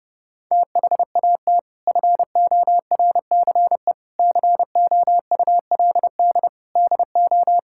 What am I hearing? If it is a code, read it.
Morse code, heard as a keyed tone, about 32 wpm, 700 Hz. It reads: THAT FORCE COULD DO
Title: Sentences from Top 1000 Words - Review Part 5 - 30wpm